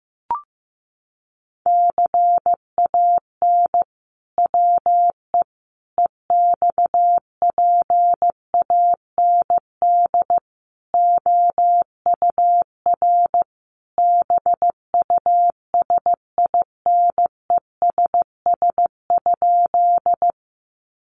Morse code, heard as a keyed tone, about 15 wpm, 700 Hz. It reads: CAN WE EXPAND OUR BUSINESS?
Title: Sentences from Top 1400 Words 15wpm